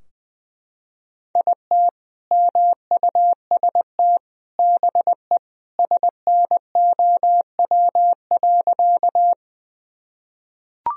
Morse code, heard as a keyed tone, about 20 wpm, 700 Hz. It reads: IT MUST BE SNOW.
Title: Sentences from Top 1000 Words - Review Part 3 - 20wpm